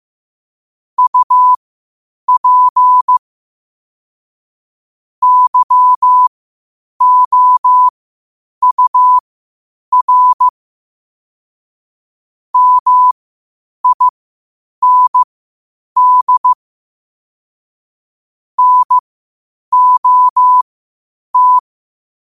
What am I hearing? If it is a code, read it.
Morse code, heard as a keyed tone, about 15 wpm, spaced as for 8 wpm, 1000 Hz. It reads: UP YOUR MIND NOT
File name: qotdfiles / 8wpm